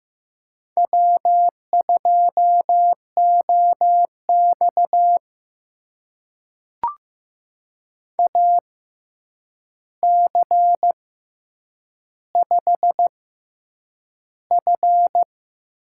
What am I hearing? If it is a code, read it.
Morse code, heard as a keyed tone, about 15 wpm, spaced as for 8 wpm, 700 Hz. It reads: W2OX AC5F
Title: USA Call Signs - 4 Characters - 6x Character Spacing - 15wpm